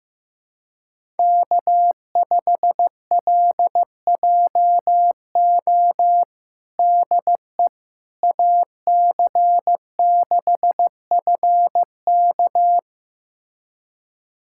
Morse code, heard as a keyed tone, about 15 wpm, 700 Hz. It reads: K5LJO DE AC6FK
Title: USA Callsign DE Callsign - 5 Characters - 15wpm